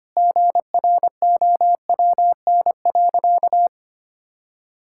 Morse code, heard as a keyed tone, about 25 wpm, 700 Hz. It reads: GROWN.
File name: practice / 25wpm / Sentences with 4-Word Phrases - 25wpm